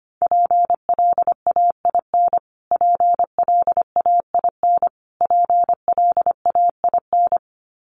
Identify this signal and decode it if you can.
Morse code, heard as a keyed tone, about 25 wpm, 700 Hz. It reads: PLAIN PLAIN PLAIN
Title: Top 500 Words Repeated 25wpm